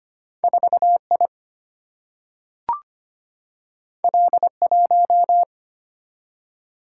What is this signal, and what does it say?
Morse code, heard as a keyed tone, about 25 wpm, 700 Hz. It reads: 4I L1
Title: Single-Letter Single-Number 25wpm